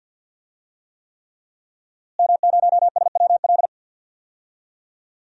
Morse code, heard as a keyed tone, about 52 wpm, 700 Hz. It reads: M0SWL
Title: DXCC-100 Call Signs - 5 Characters - Repeated 3x - 50wpm